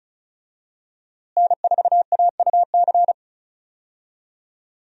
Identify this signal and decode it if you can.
Morse code, heard as a keyed tone, about 36 wpm, 700 Hz. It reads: N4AUC
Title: USA Call Signs - 5 Characters - Repeated 2x - 35wpm